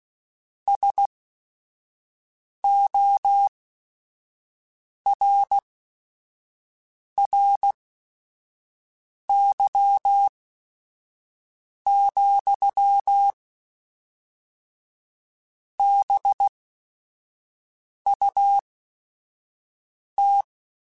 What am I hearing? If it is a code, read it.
Morse code, heard as a keyed tone, about 16 wpm, spaced as for 5 wpm, 800 Hz. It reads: SORRY, BUT